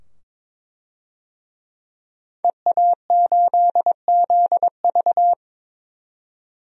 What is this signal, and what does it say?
Morse code, heard as a keyed tone, about 22 wpm, 700 Hz. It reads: EA8ZV
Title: DXCC-100 Call Signs - 5 Characters - 22wpm